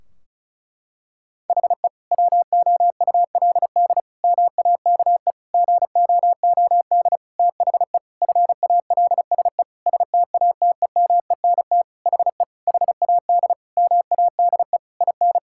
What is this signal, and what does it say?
Morse code, heard as a keyed tone, about 36 wpm, 700 Hz. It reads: HE WOULD MAKE GOOD THE FALSE STATEMENT HE HAD MADE IN
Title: Sentences from Top 1600 Words 35wpm